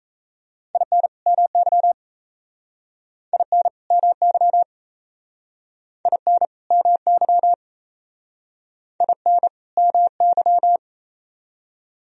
Morse code, heard as a keyed tone, about 36 wpm, 700 Hz. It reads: IN MY IN MY IN MY IN MY